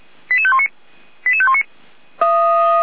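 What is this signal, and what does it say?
Five-tone selective call (ZVEI), heard as quick runs of five tones, tones of about 70 ms. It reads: 88519 88519
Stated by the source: ZVEI five-tone selective call received on an SDR, fire brigade.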